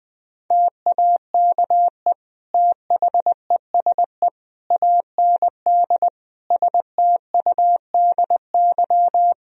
Morse code, heard as a keyed tone, about 20 wpm, 700 Hz. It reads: TAKE THESE AND STUDY
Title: Sentences from Top 1000 Words - Review Part 1 - 20wpm